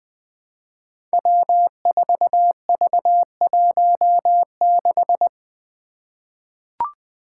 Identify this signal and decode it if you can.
Morse code, heard as a keyed tone, about 20 wpm, 700 Hz. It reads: W4V16
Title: Five Challenging Letter-Number Combinations 20wpm